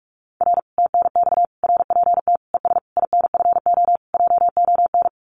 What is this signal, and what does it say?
Morse code, heard as a keyed tone, about 44 wpm, 700 Hz. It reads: R TNX RPT ES INFO JON
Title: Common QSO Phrases Encore 45wpm